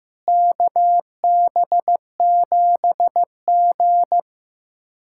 Morse code, heard as a keyed tone, about 15 wpm, 700 Hz. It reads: KB7G
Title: USA Callsign DE Callsign - 4 Characters - 15wpm